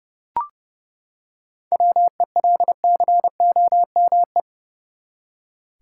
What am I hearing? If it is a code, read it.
Morse code, heard as a keyed tone, about 30 wpm, 700 Hz. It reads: WELCOME
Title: Seven Letter Words - 30wpm